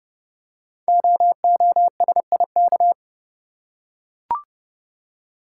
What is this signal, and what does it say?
Morse code, heard as a keyed tone, about 29 wpm, 700 Hz. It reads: OOSIK
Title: Code Groups - 5 Letters - 30wpm